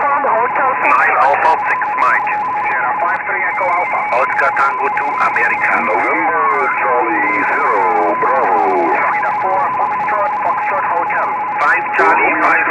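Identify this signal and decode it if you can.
Two-tone teleprinter signal, about 170 Hz shift, with speech.
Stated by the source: RTTY copying test at the BCC Challenge, a contest skills competition held at the HAM Radio fair in Friedrichshafen.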